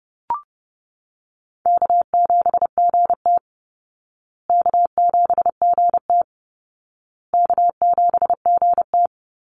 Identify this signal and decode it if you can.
Morse code, heard as a keyed tone, about 30 wpm, 700 Hz. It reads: K7GT K7GT K7GT
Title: Easy Call Signs Repeated 3x - 30wpm